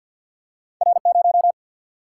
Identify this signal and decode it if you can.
Morse code, heard as a keyed tone, about 50 wpm, 700 Hz. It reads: R0
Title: Single-Letter Single-Number 50wpm